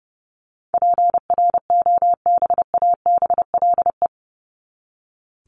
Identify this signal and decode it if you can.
Morse code, heard as a keyed tone, about 30 wpm, 700 Hz. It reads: PROBABLE
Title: LY Suffix - Top 100 Words with Base Words 30wpm